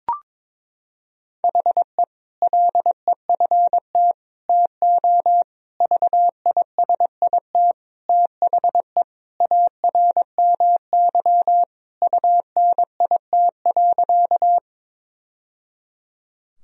Morse code, heard as a keyed tone, about 22 wpm, 700 Hz. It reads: HE LEFT TO VISIT THE ARMY UNIT.
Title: Sentences from Top 1000 Words - Review Part 2 - 22wpm